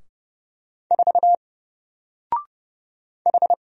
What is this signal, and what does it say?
Morse code, heard as a keyed tone, about 29 wpm, 700 Hz. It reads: 4 H
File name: Single-Letter-Number-30wpm